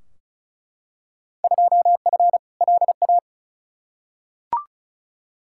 Morse code, heard as a keyed tone, about 36 wpm, 700 Hz. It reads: 2F LA
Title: Field Day Exchanges - 35wpm